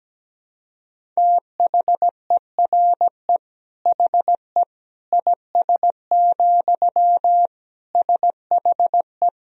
Morse code, heard as a keyed tone, about 17 wpm, 700 Hz. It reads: THERE HE IS, SHE
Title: Sentences from Top 100 Words and Abbreviations 17wpm